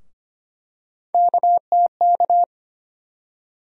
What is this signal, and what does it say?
Morse code, heard as a keyed tone, about 25 wpm, 700 Hz. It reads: KTK
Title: USA Call Sign Postfixes - 3 Letters 25wpm